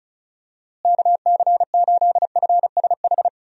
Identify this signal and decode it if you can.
Morse code, heard as a keyed tone, about 36 wpm, 700 Hz. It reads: KC8FSH
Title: USA Call Signs - 6 Characters - Repeated 2x - 35wpm